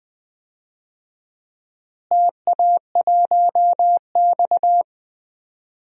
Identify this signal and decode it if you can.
Morse code, heard as a keyed tone, about 20 wpm, 700 Hz. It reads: TA1X
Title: DXCC-100 Call Signs - 4 Characters - Repeated 3x - 20wpm